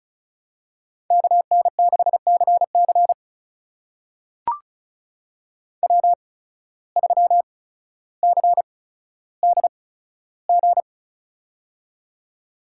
Morse code, heard as a keyed tone, about 34 wpm, spaced as for 15 wpm, 690 Hz. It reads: KN6CC W3CDG
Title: USA Call Signs - 5 Characters - 8x Character Spacing - 35wpm